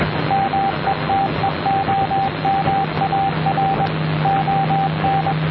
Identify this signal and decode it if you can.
Morse code, heard as a keyed tone, about 21 wpm, 780 Hz, with radio noise. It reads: METEOMAR ON